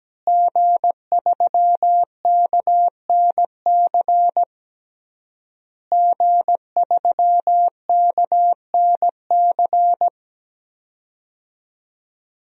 Morse code, heard as a keyed tone, about 17 wpm, 700 Hz. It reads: G3KNC G3KNC
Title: DXCC-100 Call Signs - 5 Characters - Repeated 3x - 17wpm